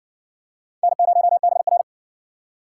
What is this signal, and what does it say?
Morse code, heard as a keyed tone, about 53 wpm, 700 Hz. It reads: N0BR